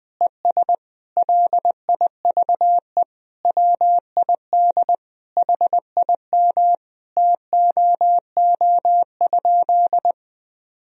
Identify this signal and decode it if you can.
Morse code, heard as a keyed tone, about 20 wpm, 700 Hz. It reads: ES LIVE WID HIM TOO?
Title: Sentences from Top 500 Words and Abbreviations 20wpm